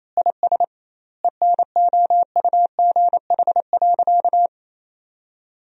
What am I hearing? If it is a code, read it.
Morse code, heard as a keyed tone, about 28 wpm, 700 Hz. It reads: IS ENOUGH.
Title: 2x Word-Spacing - Sentences from Top 400 Words - 28wpm